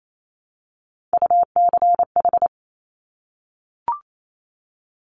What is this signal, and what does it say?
Morse code, heard as a keyed tone, about 28 wpm, 700 Hz. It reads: UCH
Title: Top Three Letter Combinations in Words 28wpm